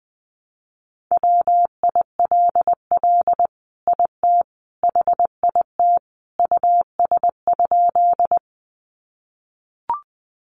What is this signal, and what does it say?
Morse code, heard as a keyed tone, about 20 wpm, 700 Hz. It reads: WILL IT HIT US?
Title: Sentences from Top 1200 Words 20wpm